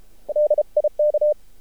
Morse code, heard as a keyed tone, about 32 wpm, 600 Hz. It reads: DIK
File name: archived / CRSnet